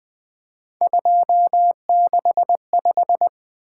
Morse code, heard as a keyed tone, about 24 wpm, 700 Hz. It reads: 265
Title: Speed-Racing - Three Letter-Number Combinations - 20wpm